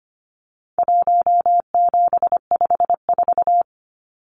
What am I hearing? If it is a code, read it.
Morse code, heard as a keyed tone, about 25 wpm, 700 Hz. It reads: 1754
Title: Numbers - Contest Style - 4-Digit 25wpm